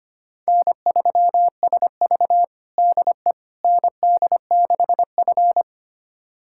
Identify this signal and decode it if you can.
Morse code, heard as a keyed tone, about 25 wpm, 700 Hz. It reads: N3SV DE ND6F
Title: USA Callsign DE Callsign - 4 Characters - 25wpm